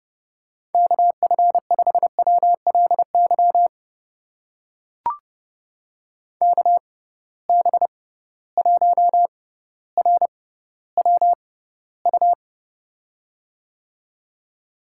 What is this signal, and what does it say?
Morse code, heard as a keyed tone, about 30 wpm, spaced as for 15 wpm, 700 Hz. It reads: KF5WLY KB1RWU
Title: USA Call Signs - 6 Characters - 6x Character Spacing - 30wpm